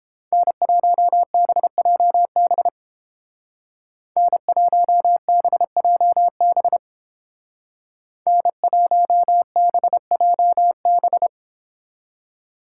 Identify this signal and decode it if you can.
Morse code, heard as a keyed tone, about 30 wpm, 700 Hz. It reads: N1BJB N1BJB N1BJB